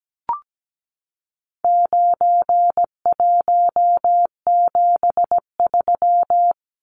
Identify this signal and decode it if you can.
Morse code, heard as a keyed tone, about 17 wpm, 700 Hz. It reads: 9173
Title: Numbers - 4-Digit 17wpm